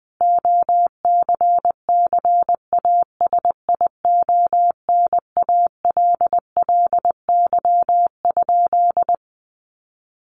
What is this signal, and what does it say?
Morse code, heard as a keyed tone, about 20 wpm, 700 Hz. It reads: OCCASIONALLY?
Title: Sentences from Top 1300 Words 20wpm